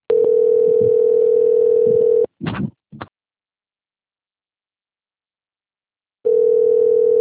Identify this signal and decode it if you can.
Ringback tone (North America), 440 + 480 Hz.